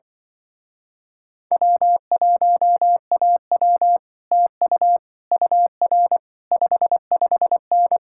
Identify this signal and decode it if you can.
Morse code, heard as a keyed tone, about 25 wpm, 700 Hz. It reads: W1AW TU UR 55N